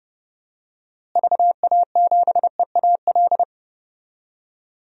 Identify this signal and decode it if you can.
Morse code, heard as a keyed tone, about 30 wpm, 700 Hz. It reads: VA7EAL
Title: DXCC-100 Call Signs - 6 Characters - Repeated 3x - 30wpm